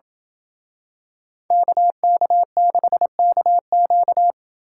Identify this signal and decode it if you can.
Morse code, heard as a keyed tone, about 27 wpm, 700 Hz. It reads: KK6KQ